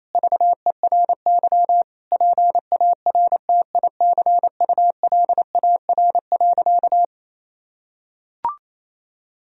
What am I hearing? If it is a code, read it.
Morse code, heard as a keyed tone, about 28 wpm, 700 Hz. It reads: VERY PARTICULAR.
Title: Sentences from Top 1000 Words - Review Part 4 - 28wpm